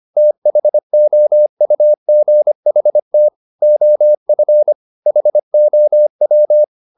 Morse code, heard as a keyed tone, about 25 wpm, 600 Hz. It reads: THOUGHT OF HOW
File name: the old man and the sea part 1 25wpm